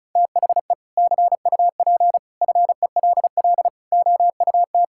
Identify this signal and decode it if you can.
Morse code, heard as a keyed tone, about 34 wpm, 700 Hz. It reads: THE CUP FELL OUT
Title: Sentences from Top 1600 Words 35wpm